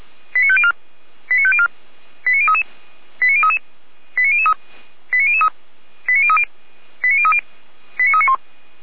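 Five-tone selective call (ZVEI), heard as quick runs of five tones, tones of about 70 ms. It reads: 89504 89504 89033 89033 89003 89003 89030 89030 89391
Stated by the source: SDR-received ZVEI five-tone selective call.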